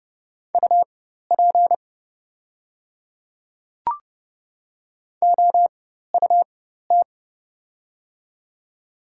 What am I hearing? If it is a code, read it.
Morse code, heard as a keyed tone, about 29 wpm, spaced as for 14 wpm, 700 Hz. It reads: UP OUT
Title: 4x Character Spacing - Top 200 Words - 30wpm